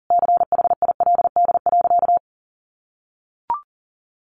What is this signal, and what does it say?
Morse code, heard as a keyed tone, about 40 wpm, 700 Hz. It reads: CHILD.
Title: Sentences from Top 1400 Words 40wpm